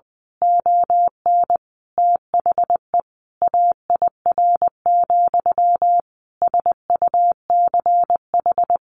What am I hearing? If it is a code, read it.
Morse code, heard as a keyed tone, about 20 wpm, 700 Hz. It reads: ON THE AIR, SUCH
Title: A Sound Of Thunder 20wpm